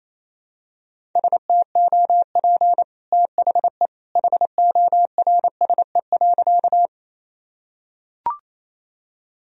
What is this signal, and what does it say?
Morse code, heard as a keyed tone, about 28 wpm, 700 Hz. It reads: STOP THE HORSE.
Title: Sentences from Top 1000 Words - Review Part 1 - 28wpm